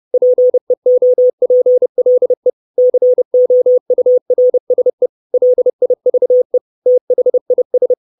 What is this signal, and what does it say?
Morse code, heard as a keyed tone, about 30 wpm, 500 Hz. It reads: PEOPLE COURSE LIVE THIS
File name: 632 Words 30WPM